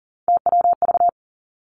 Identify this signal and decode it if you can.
Morse code, heard as a keyed tone, about 42 wpm, 700 Hz. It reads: TWV
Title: Three Challenging Letter-Number Combinations 40wpm